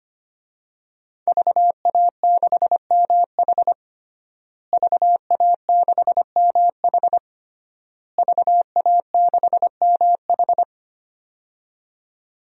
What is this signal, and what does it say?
Morse code, heard as a keyed tone, about 25 wpm, 700 Hz. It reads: VA6MH VA6MH VA6MH
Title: DXCC-100 Call Signs - 5 Characters - Repeated 3x - 25wpm